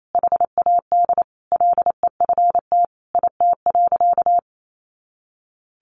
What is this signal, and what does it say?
Morse code, heard as a keyed tone, about 28 wpm, 700 Hz. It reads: HAD LEFT IT.
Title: A Sound Of Thunder 28wpm